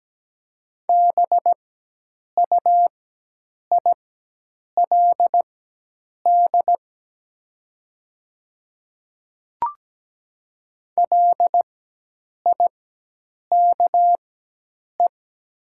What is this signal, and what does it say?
Morse code, heard as a keyed tone, about 17 wpm, spaced as for 8 wpm, 700 Hz. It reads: BUILD LIKE
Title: 4x Character Spacing - Top 200 Words - 17wpm